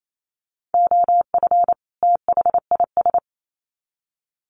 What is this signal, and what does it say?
Morse code, heard as a keyed tone, about 28 wpm, 700 Hz. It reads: OF THIS
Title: Sets of 2 Words 28wpm